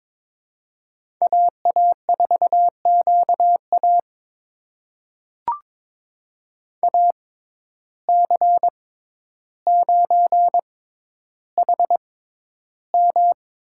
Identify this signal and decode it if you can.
Morse code, heard as a keyed tone, about 21 wpm, spaced as for 11 wpm, 700 Hz. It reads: AA4QA AC9HM
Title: USA Call Signs - 5 Characters - 6x Character Spacing - 22wpm